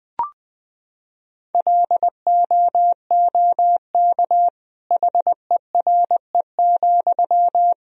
Morse code, heard as a keyed tone, about 20 wpm, 700 Hz. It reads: LOOK HERE,
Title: Sentences from Top 1000 Words - Review Part 2 - 20wpm